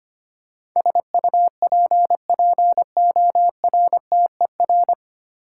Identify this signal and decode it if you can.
Morse code, heard as a keyed tone, about 25 wpm, 690 Hz. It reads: SUPPORTER
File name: practice / 25wpm / 9 Letter Words - First 3 Letters Given - 25wpm